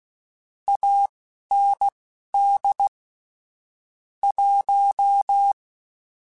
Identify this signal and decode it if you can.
Morse code, heard as a keyed tone, about 16 wpm, spaced as for 11 wpm, 800 Hz. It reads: AND 1